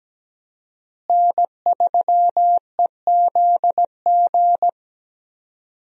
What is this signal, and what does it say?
Morse code, heard as a keyed tone, about 17 wpm, 700 Hz. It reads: N3EZG
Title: USA Call Signs - 5 Characters - Repeated 3x - 17wpm